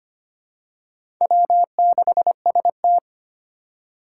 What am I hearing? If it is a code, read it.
Morse code, heard as a keyed tone, about 25 wpm, 700 Hz. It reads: W6ST
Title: USA Call Signs - 4 Characters - 3x Character Spacing - 25wpm